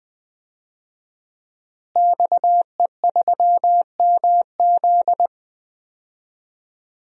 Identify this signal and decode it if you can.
Morse code, heard as a keyed tone, about 20 wpm, 700 Hz. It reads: XE3MZ
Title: DXCC-100 Call Signs - 5 Characters - Repeated 2x - 20wpm